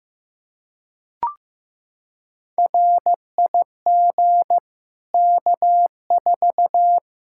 Morse code, heard as a keyed tone, about 16 wpm, 700 Hz. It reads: RIG K4
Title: Practice QSO - Exchange 2 - Info - 15wpm